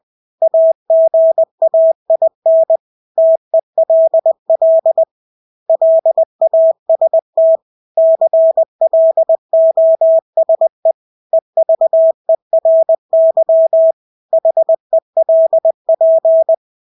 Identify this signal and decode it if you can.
Morse code, heard as a keyed tone, about 20 wpm, 650 Hz. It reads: AGAIN TELL LAST CLOSE EVERY HELP
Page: Engelske ord 20 wpm